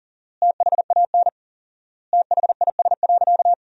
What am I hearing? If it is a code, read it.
Morse code, heard as a keyed tone, about 40 wpm, 700 Hz. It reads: THAN THIS.
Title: 4x Word-Spacing - Sentences from Top 100 Words - 40wpm